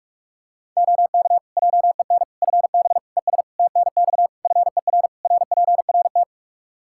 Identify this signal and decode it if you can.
Morse code, heard as a keyed tone, about 41 wpm, 700 Hz. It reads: OK JEN FB ES TNX FER RPRT
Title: Practice QSO - Exchange 2 - Info - 45wpm